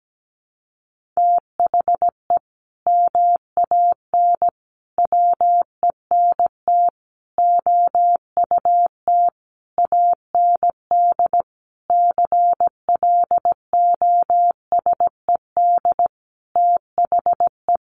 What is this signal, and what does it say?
Morse code, heard as a keyed tone, about 17 wpm, 700 Hz. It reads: THE MAN WENT OUT AND CLOSED THE
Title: Sentences from Top 1400 Words 17wpm